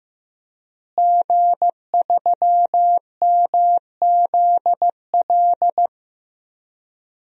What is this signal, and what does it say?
Morse code, heard as a keyed tone, about 15 wpm, 700 Hz. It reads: G3MZL
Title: DXCC-100 Call Signs - 5 Characters - Repeated 3x - 15wpm